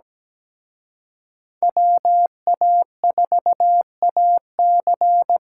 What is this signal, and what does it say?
Morse code, heard as a keyed tone, about 20 wpm, 700 Hz. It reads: WA4AC
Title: Speed Racing - USA 2x2 Call Signs 17wpm